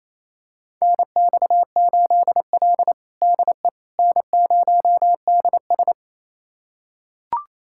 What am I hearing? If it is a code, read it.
Morse code, heard as a keyed tone, about 27 wpm, 700 Hz. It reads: NX8L DE N0DS
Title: USA Callsign DE Callsign - 4 Characters - 28wpm